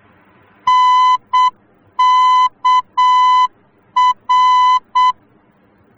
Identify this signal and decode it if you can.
Morse code, heard as a keyed tone, about 7 wpm, 1010 Hz. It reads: NKR